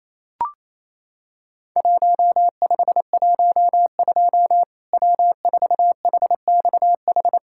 Morse code, heard as a keyed tone, about 28 wpm, 700 Hz. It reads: 1512 W4HXH
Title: Worked All Europe DX Contest - QTC Exchange - 28wpm